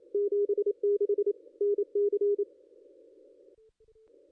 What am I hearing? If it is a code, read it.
Morse code, heard as a keyed tone, about 27 wpm, 410 Hz. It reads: 76 NC
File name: contests / ss-cw-k5tr-op-n5rz -0200